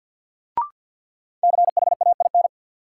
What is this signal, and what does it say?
Morse code, heard as a keyed tone, about 50 wpm, 700 Hz. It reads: CHAIN
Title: Sentences from Top 1500 Words - New Word Drill Alt - 50wpm